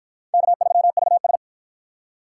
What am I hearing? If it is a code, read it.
Morse code, heard as a keyed tone, about 53 wpm, 700 Hz. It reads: K3VS